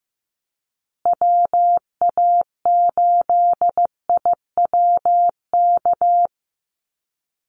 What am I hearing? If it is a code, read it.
Morse code, heard as a keyed tone, about 15 wpm, 700 Hz. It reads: WA8IWK
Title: Call Signs Repeated 2x - 15wpm